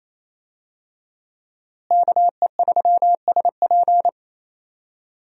Morse code, heard as a keyed tone, about 28 wpm, 700 Hz. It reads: KE3SP
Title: USA Call Signs - 5 Characters - Repeated 3x - 28wpm